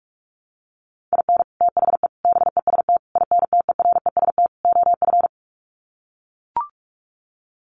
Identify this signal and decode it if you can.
Morse code, heard as a keyed tone, about 44 wpm, 700 Hz. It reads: IN THE BEST INTEREST OF
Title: Sets of 5 Words Encore 45wpm